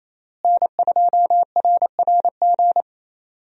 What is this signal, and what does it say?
Morse code, heard as a keyed tone, about 28 wpm, 700 Hz. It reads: N2RRG
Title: USA Call Signs - 5 Characters - Repeated 3x - 28wpm